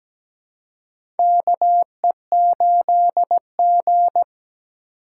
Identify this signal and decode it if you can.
Morse code, heard as a keyed tone, about 17 wpm, 700 Hz. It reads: KE8G
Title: USA Call Signs - 4x Character Spacing - 17wpm